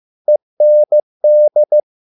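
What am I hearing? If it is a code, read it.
Morse code, heard as a keyed tone, about 15 wpm, 600 Hz. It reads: END
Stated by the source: Morse code at a slow 15 wpm from CW med Gnister.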